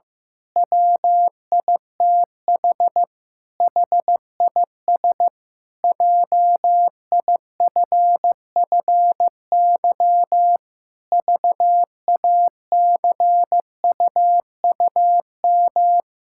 Morse code, heard as a keyed tone, about 16 wpm, 700 Hz. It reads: WITH HIS JIFFY VACUUM
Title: Pangram Challenge v2 - 15wpm